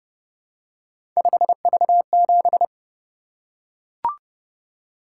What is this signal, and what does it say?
Morse code, heard as a keyed tone, about 28 wpm, 700 Hz. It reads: 5V7